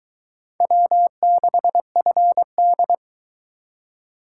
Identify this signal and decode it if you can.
Morse code, heard as a keyed tone, about 21 wpm, 700 Hz. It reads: W6FD